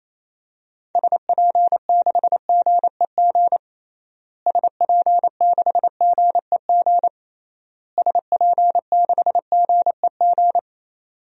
Morse code, heard as a keyed tone, about 28 wpm, 700 Hz. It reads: SP6GEG SP6GEG SP6GEG
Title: DXCC-100 Call Signs - 6 Characters - Repeated 3x - 28wpm